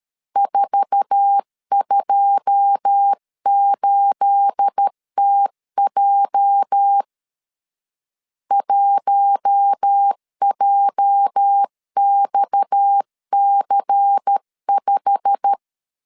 Morse code, heard as a keyed tone, about 13 wpm, 800 Hz. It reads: 428TJ 1JXC5
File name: LEZIONI / 9 28 12wpm 2351